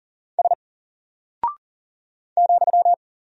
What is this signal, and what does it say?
Morse code, heard as a keyed tone, about 39 wpm, 700 Hz. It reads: S ,
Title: Single Character (Letter:Number:Punctuation:Prosign) 40wpm